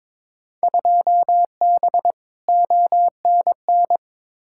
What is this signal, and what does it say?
Morse code, heard as a keyed tone, about 22 wpm, 700 Hz. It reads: 2B ONN
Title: Field Day Exchanges Encore - 22wpm